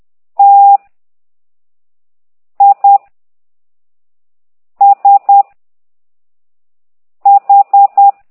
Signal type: Morse code